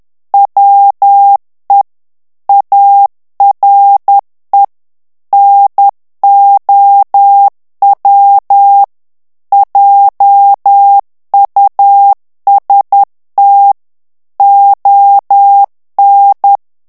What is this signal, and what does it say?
Morse code, heard as a keyed tone, about 10 wpm, 800 Hz. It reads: WE ARE NOW JUST ON
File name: morse mp3 / ANGLAIS texte clair 2 vitesse 10 mots minute